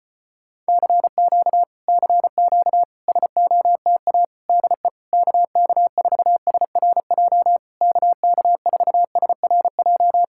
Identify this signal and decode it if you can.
Morse code, heard as a keyed tone, about 35 wpm, 700 Hz. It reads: CQ CQ SOTA DE KK4SRJ KK4SRJ